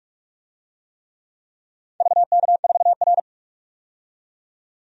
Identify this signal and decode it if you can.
Morse code, heard as a keyed tone, about 46 wpm, 700 Hz. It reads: VK4R
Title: DXCC-100 Call Signs - 4 Characters - 45wpm